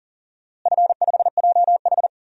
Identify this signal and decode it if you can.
Morse code, heard as a keyed tone, about 37 wpm, 700 Hz. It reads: F5JH
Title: DXCC-100 Call Signs - 4 Characters - Repeated 2x - 40wpm